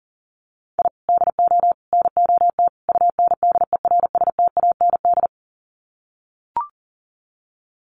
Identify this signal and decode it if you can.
Morse code, heard as a keyed tone, about 40 wpm, 700 Hz. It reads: I DO NOT UNDERSTAND
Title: Sets of 4 Words 40wpm